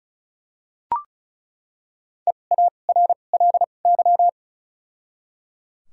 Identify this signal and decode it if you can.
Morse code, heard as a keyed tone, about 36 wpm, 700 Hz. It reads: E A R L Y
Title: 2x Character Spacing - Top 500 Words - 35wpm